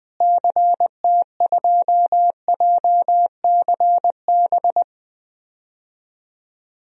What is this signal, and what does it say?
Morse code, heard as a keyed tone, about 21 wpm, 700 Hz. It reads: CT2JCB
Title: DXCC-100 Call Signs - 6 Characters - Repeated 2x - 20wpm